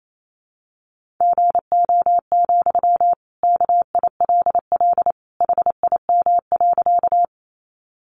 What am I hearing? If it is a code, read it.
Morse code, heard as a keyed tone, about 28 wpm, 700 Hz. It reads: GO, KILL HIM.
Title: Sentences from Top 1400 Words 28wpm